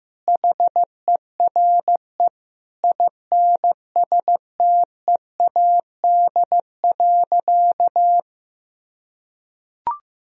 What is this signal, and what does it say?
Morse code, heard as a keyed tone, about 15 wpm, 700 Hz. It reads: HERE INSTEAD.
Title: Sentences from Top 1000 Words - Review Part 4 - 15wpm